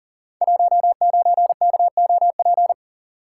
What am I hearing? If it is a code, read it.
Morse code, heard as a keyed tone, about 40 wpm, 700 Hz. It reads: 19KOP
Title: Code Groups - 5 Letter:Numbers - 40wpm